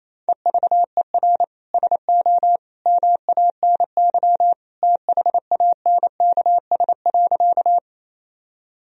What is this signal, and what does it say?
Morse code, heard as a keyed tone, about 28 wpm, 700 Hz. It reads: EVER SO MANY THANKS.
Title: Sentences from Top 1000 Words - Review Part 3 - 28wpm